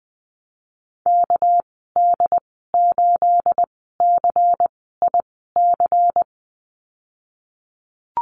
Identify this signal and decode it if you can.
Morse code, heard as a keyed tone, about 20 wpm, spaced as for 15 wpm, 700 Hz. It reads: KD8CIC
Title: 245 Lesson 39g - Letter z - Call Signs 20wpm